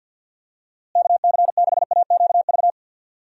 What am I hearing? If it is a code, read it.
Morse code, heard as a keyed tone, about 51 wpm, 700 Hz. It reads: KC6AQV
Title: USA Call Signs - 6 Characters - Repeated 2x - 50wpm